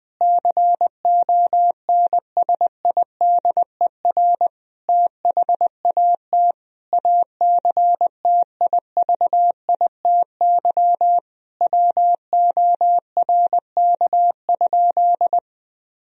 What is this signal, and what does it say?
Morse code, heard as a keyed tone, about 21 wpm, 700 Hz. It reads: CONSIDER THAT ACTIVITY WORK?
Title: Sentences from Top 1000 Words - Review Part 4 - 20wpm